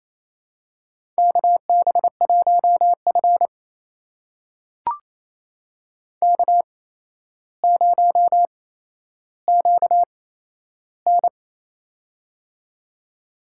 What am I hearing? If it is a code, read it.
Morse code, heard as a keyed tone, about 28 wpm, spaced as for 12 wpm, 700 Hz. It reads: KB1F K0QN
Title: USA Call Signs - 4 Characters - 8x Character Spacing - 28wpm